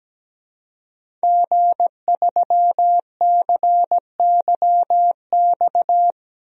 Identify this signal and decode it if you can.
Morse code, heard as a keyed tone, about 17 wpm, 700 Hz. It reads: G3CYX
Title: DXCC-100 Call Signs - 5 Characters - 17wpm